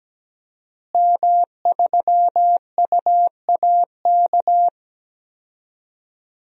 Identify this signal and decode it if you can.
Morse code, heard as a keyed tone, about 17 wpm, 700 Hz. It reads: M3UAK
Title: DXCC-100 Call Signs - 5 Characters - Repeated 2x - 17wpm